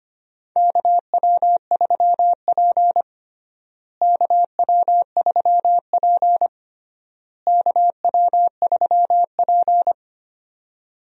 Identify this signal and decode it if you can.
Morse code, heard as a keyed tone, about 25 wpm, 700 Hz. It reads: KW3P KW3P KW3P
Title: USA Call Signs - 4 Characters - Repeated 3x - 25wpm